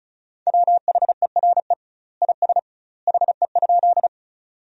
Morse code, heard as a keyed tone, about 36 wpm, 700 Hz. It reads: WHERE IS HE?
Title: 2x Word-Spacing - Sentences from Top 200 Words - 35wpm